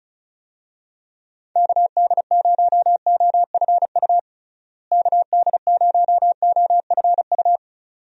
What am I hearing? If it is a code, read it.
Morse code, heard as a keyed tone, about 36 wpm, 700 Hz. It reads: KD0OFU KD0OFU
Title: USA Call Signs - 6 Characters - Repeated 2x - 35wpm